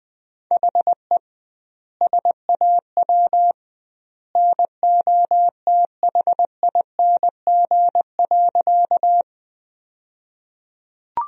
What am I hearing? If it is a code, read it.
Morse code, heard as a keyed tone, about 21 wpm, 700 Hz. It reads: HE SAW NOTHING.
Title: 2x Word-Spacing - Sentences from Top 500 Words - 20wpm